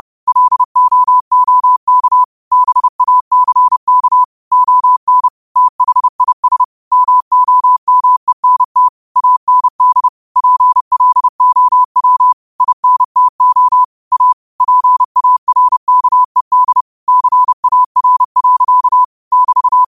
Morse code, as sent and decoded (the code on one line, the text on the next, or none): .-.. --- --- -.- / -... .- -.-. -.- / --- -. / - .... .. ... / -- --- -- . -. - / .- -. -.. / .--. .-.. --- .-- / .. -. - --- / .- / .--. .- .-. -.- . -.. / -.-. .- .-. .-.-.- / -...-
LOOK BACK ON THIS MOMENT AND PLOW INTO A PARKED CAR. <BT>